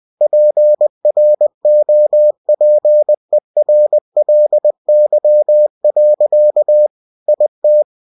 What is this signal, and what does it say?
Morse code, heard as a keyed tone, about 20 wpm, 600 Hz. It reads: PROPERLY. IT